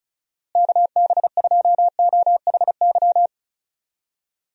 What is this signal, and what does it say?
Morse code, heard as a keyed tone, about 34 wpm, 700 Hz. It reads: KB2OHY
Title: USA Call Signs - 6 Characters - Repeated 3x - 35wpm